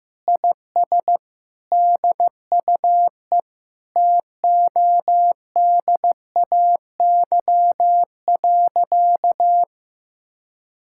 Morse code, heard as a keyed tone, about 16 wpm, 700 Hz. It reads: IS DUE TODAY.
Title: Sentences from Top 1000 Words - Review Part 6 - 15wpm